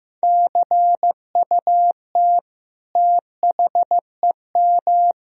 Morse code, heard as a keyed tone, about 15 wpm, 700 Hz. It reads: CUT THEM
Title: Sentences from Top 1000 Words - Review Part 1 - 15wpm